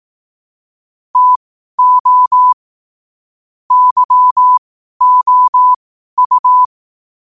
Morse code, heard as a keyed tone, about 18 wpm, spaced as for 12 wpm, 1000 Hz. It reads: TO YOU